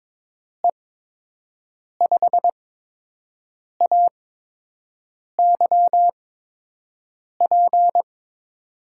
Morse code, heard as a keyed tone, about 22 wpm, 700 Hz. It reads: E 5 A Y P
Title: USA Call Signs - 6 Characters - 8x Character Spacing - 22wpm